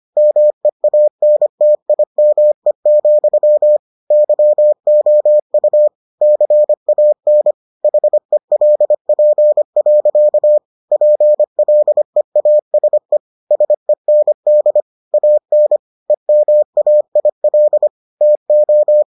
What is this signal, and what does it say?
Morse code, heard as a keyed tone, about 25 wpm, 600 Hz. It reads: MEANTIME, YOU CAN HELP. PLEASE SEND AN EMAIL TO